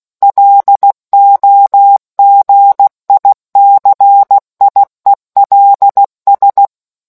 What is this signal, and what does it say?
Morse code, heard as a keyed tone, about 16 wpm, 800 Hz. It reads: LOGICIELS